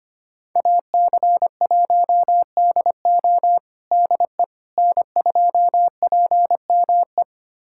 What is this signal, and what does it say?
Morse code, heard as a keyed tone, about 25 wpm, 700 Hz. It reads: AC1DO DE N2PME
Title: USA Callsign DE Callsign - 5 Characters - 25wpm